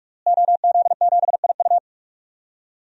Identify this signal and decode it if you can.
Morse code, heard as a keyed tone, about 42 wpm, 700 Hz. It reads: OZ7IU
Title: DXCC-100 Call Signs - 5 Characters - Repeated 2x - 45wpm